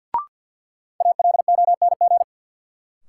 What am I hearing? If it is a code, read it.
Morse code, heard as a keyed tone, about 53 wpm, 700 Hz. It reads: ALONG
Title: Sentences from Top 700 Words - New Word Drill - 50wpm